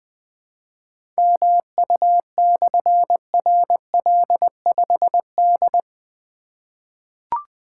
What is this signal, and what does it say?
Morse code, heard as a keyed tone, about 20 wpm, 700 Hz. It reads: MU/RL5D
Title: Difficult Call Signs 20wpm